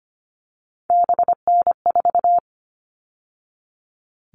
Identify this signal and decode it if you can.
Morse code, heard as a keyed tone, about 25 wpm, 700 Hz. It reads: BN4
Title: All Cut Numbers - 3-Digit 25wpm